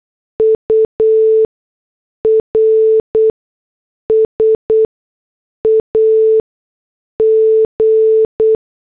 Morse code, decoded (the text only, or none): URS AG